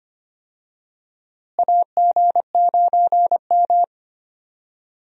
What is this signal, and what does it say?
Morse code, heard as a keyed tone, about 25 wpm, 700 Hz. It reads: AG9M
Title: USA Call Signs - 4 Characters - Repeated 2x - 25wpm